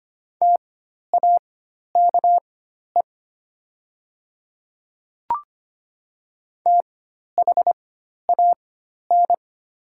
Morse code, heard as a keyed tone, about 25 wpm, spaced as for 12 wpm, 700 Hz. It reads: TAKE THAN
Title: 4x Character Spacing - Top 500 Words - 25wpm